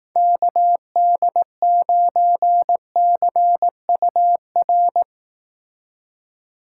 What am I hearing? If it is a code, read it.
Morse code, heard as a keyed tone, about 18 wpm, 700 Hz. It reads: KD9CUR
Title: Speed Racing - USA 2x3 Call Signs 15wpm